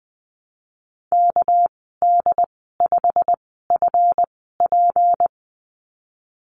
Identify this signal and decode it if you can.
Morse code, heard as a keyed tone, about 20 wpm, spaced as for 15 wpm, 700 Hz. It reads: KD5FP
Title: 219 Lesson 35g - Number 0 - Call Signs 20wpm